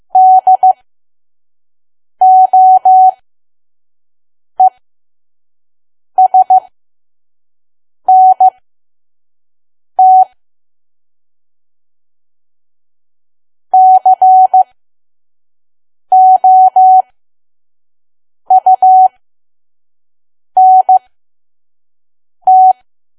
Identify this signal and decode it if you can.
Morse code, heard as a keyed tone, about 15 wpm, spaced as for 5 wpm, 750 Hz. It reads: DOESNT COUNT
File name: Archive / 5 WPM files / PM